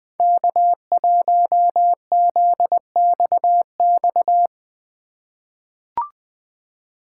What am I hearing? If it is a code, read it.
Morse code, heard as a keyed tone, about 21 wpm, 700 Hz. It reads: K1ZXX
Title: USA Call Signs - 5 Characters - 3x Character Spacing - 20wpm